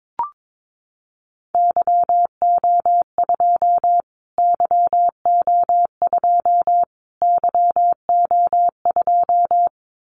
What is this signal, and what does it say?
Morse code, heard as a keyed tone, about 22 wpm, 700 Hz. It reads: YO2 YO2 YO2
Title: Callsign Prefixes for Top 25 DXCC - Repeated 3x - 22wpm